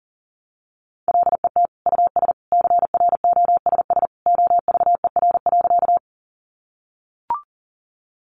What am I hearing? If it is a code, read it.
Morse code, heard as a keyed tone, about 40 wpm, 700 Hz. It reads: LET US CROSS OVER.
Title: Sentences from Top 300 Words 40wpm